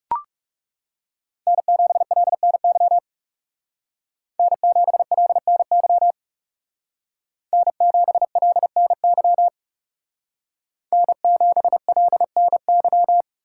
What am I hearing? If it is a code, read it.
Morse code, heard as a keyed tone, about 39 wpm, 700 Hz. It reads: N7LNY N7LNY N7LNY N7LNY